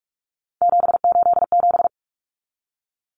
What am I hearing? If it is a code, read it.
Morse code, heard as a keyed tone, about 44 wpm, 700 Hz. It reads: 787
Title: Common Cut Numbers - 3-Digit 45wpm